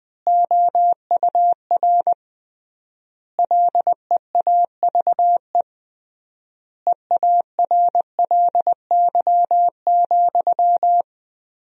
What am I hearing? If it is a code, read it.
Morse code, heard as a keyed tone, about 20 wpm, 700 Hz. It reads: OUR LEAVE EARLY,
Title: 3x Word-Spacing - Sentences from Top 400 Words - 20wpm